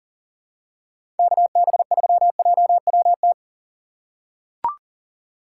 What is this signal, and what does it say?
Morse code, heard as a keyed tone, about 39 wpm, 700 Hz. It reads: KB3JWT
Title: USA Call Signs - 3x Character Spacing - 40wpm